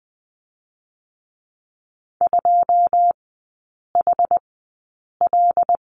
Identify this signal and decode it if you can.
Morse code, heard as a keyed tone, about 20 wpm, 690 Hz. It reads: 2 H L
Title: 145 Lesson 24e - Letter y - Three Characters 20wpm